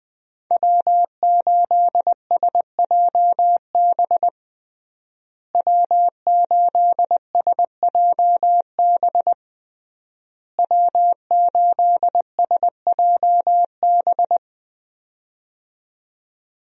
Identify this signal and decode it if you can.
Morse code, heard as a keyed tone, about 21 wpm, 700 Hz. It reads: W8SJB W8SJB W8SJB
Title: USA Call Signs - 5 Characters - Repeated 3x - 20wpm